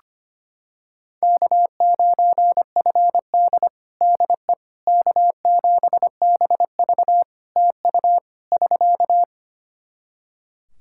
Morse code, heard as a keyed tone, about 25 wpm, 700 Hz. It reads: K9FD DE K7BV TU <SK>
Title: Practice QSO - Exchange 4 - Ending - 25wpm